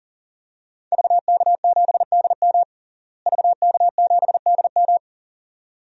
Morse code, heard as a keyed tone, about 37 wpm, 700 Hz. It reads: VK7DM VK7DM
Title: DXCC-100 Call Signs - 5 Characters - Repeated 2x - 40wpm